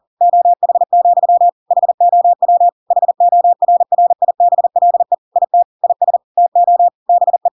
Morse code, heard as a keyed tone, about 40 wpm, 700 Hz. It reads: OH, HOW HORRIBLE IT IS TO BE
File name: Pinocchio / Pinocchio ch5 40 40 ik1hge